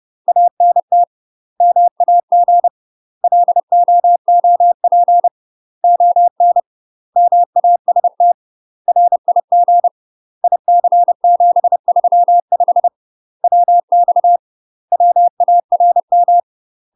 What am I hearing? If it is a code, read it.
Morse code, heard as a keyed tone, about 30 wpm, 700 Hz. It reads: ANT MAG LOOP ON MAST RIG IC735 WX WARM
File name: UK MASTER QSOs TRAINING / 30WPM / UK MASTER QSOs 0043